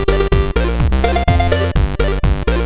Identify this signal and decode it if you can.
There is no code.